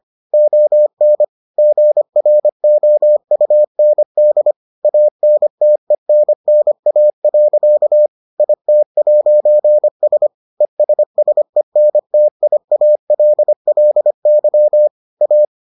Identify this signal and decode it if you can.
Morse code, heard as a keyed tone, about 25 wpm, 600 Hz. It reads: ON GROUND ANTENNA. IT'S ESSENTIALLY A